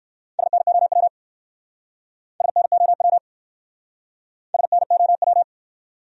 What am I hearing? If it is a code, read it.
Morse code, heard as a keyed tone, about 64 wpm, 700 Hz. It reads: SNOW SNOW SNOW